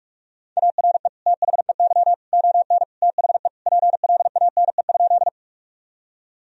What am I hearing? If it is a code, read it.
Morse code, heard as a keyed tone, about 46 wpm, 700 Hz. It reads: ARE THEY ON THE PLANE?
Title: Sentences from Top 1000 Words - Review Part 3 - 45wpm